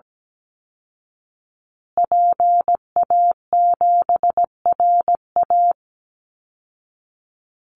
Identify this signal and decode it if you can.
Morse code, heard as a keyed tone, about 17 wpm, 700 Hz. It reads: PA7RA
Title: Call Signs Repeated 3x - 17wpm